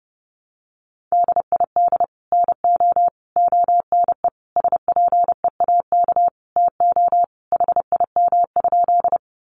Morse code, heard as a keyed tone, about 30 wpm, 700 Hz. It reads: DID NO ONE SPEAK TO HIM?
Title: Sentences from Top 1100 Words 30wpm